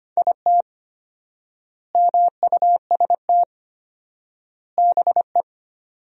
Morse code, heard as a keyed tone, about 25 wpm, 700 Hz. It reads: IT MUST BE
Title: 4x Word-Spacing - Sentences from Top 500 Words - 25wpm